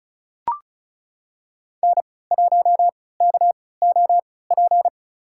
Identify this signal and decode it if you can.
Morse code, heard as a keyed tone, about 34 wpm, spaced as for 20 wpm, 700 Hz. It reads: N1KOP
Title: USA Call Signs - 5 Characters - 3x Character Spacing - 35wpm